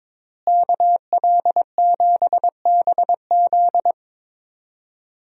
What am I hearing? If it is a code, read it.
Morse code, heard as a keyed tone, about 22 wpm, 700 Hz. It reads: KL7BZ
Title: DXCC-100 Call Signs - 5 Characters - Repeated 2x - 22wpm